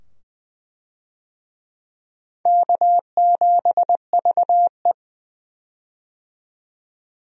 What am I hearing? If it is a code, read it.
Morse code, heard as a keyed tone, about 21 wpm, 700 Hz. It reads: K7VE
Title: USA Call Signs - 4 Characters - 20wpm